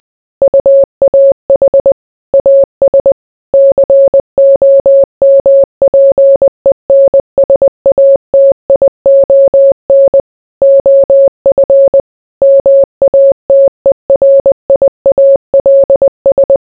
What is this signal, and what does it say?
Morse code, heard as a keyed tone, about 20 wpm, 550 Hz. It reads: UAH AS COMPENSATION OF MATERIALS